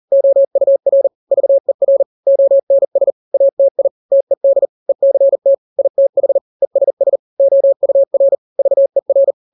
Morse code, heard as a keyed tone, about 40 wpm, 550 Hz. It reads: OUR VER ONS ATI TED ECT ITH ESS OUR VER
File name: n-grams / 3-grams 40 35 WPM